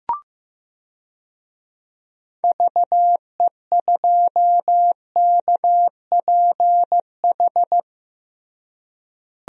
Morse code, heard as a keyed tone, about 15 wpm, 700 Hz. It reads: VE2KPH
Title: DXCC-100 Call Signs - 6 Characters - Repeated 2x - 15wpm